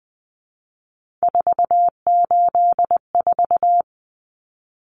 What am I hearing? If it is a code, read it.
Morse code, heard as a keyed tone, about 20 wpm, 700 Hz. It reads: 484
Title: Common Cut Numbers - 3-Digit 20wpm